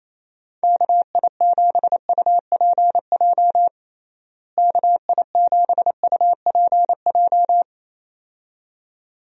Morse code, heard as a keyed tone, about 28 wpm, 700 Hz. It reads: KI7UPJ KI7UPJ
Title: USA Call Signs - 6 Characters - Repeated 2x - 28wpm